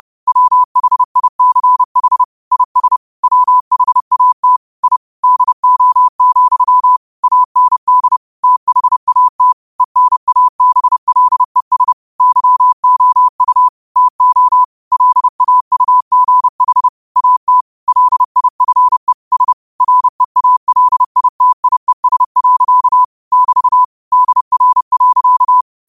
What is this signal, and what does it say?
Morse code, heard as a keyed tone, about 30 wpm, 1000 Hz. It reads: WHICH IS WHAT I DO, AND THAT ENABLES YOU TO LAUGH AT LIFE S REALITIES. <BT> DR.
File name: 30wpm